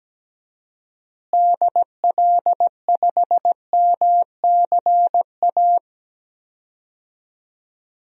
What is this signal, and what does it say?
Morse code, heard as a keyed tone, about 17 wpm, 700 Hz. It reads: DL5MCA
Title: DXCC-100 Call Signs - 6 Characters - Repeated 2x - 17wpm